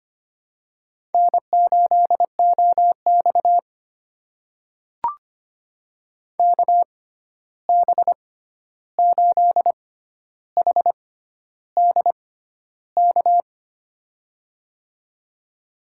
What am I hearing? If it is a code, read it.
Morse code, heard as a keyed tone, about 25 wpm, spaced as for 11 wpm, 700 Hz. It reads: N8OX KB8HDK
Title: USA Call Signs - 6x Character Spacing - 25wpm